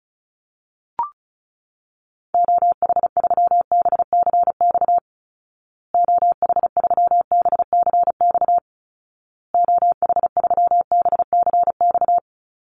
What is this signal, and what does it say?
Morse code, heard as a keyed tone, about 34 wpm, 700 Hz. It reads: OH3BCX OH3BCX OH3BCX
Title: Call Signs Repeated 2x - 35wpm